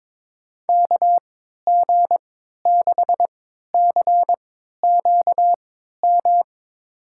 Morse code, heard as a keyed tone, about 22 wpm, spaced as for 12 wpm, 700 Hz. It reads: KG6CQM
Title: USA Call Signs - 6 Characters - 3x Character Spacing - 22wpm